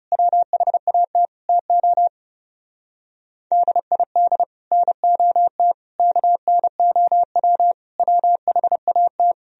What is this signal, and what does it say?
Morse code, heard as a keyed tone, about 36 wpm, 700 Hz. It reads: WHAT TO DID NOT KNOW WHAT
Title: Speed-Racing - Sets of 5 Words 30wpm